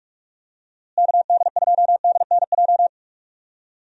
Morse code, heard as a keyed tone, about 44 wpm, 700 Hz. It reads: KD2DNJ
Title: USA Call Signs - 6 Characters - Repeated 2x - 45wpm